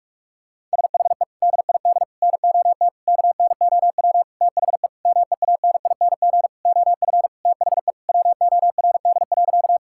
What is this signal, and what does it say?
Morse code, heard as a keyed tone, about 46 wpm, 700 Hz. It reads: SHE DID NOT KNOW THE MEANING OF THE WORD.
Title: Sentences from Top 1000 Words - Review Part 6 - 45wpm